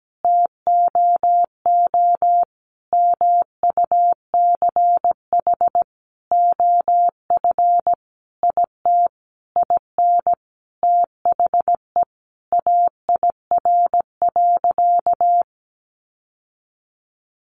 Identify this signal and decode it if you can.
Morse code, heard as a keyed tone, about 17 wpm, 700 Hz. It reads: TOO MUCH OF IT IN THE AIR.
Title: Sentences from Top 900 Words 17wpm